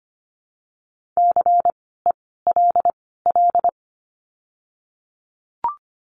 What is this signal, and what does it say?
Morse code, heard as a keyed tone, about 25 wpm, spaced as for 16 wpm, 700 Hz. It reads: CELL
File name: course / 25:10wpm / 094 Lesson 16f - Letter c - Words 25wpm